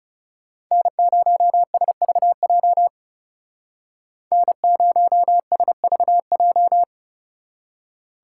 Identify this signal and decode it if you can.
Morse code, heard as a keyed tone, about 36 wpm, 700 Hz. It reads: N0SVJ N0SVJ
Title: Speed Racing - USA 1x3 Call Signs 30wpm